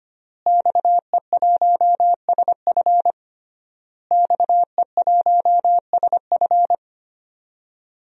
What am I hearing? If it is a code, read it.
Morse code, heard as a keyed tone, about 25 wpm, 700 Hz. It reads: XE1SF XE1SF
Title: DXCC-100 Call Signs - 5 Characters - Repeated 3x - 25wpm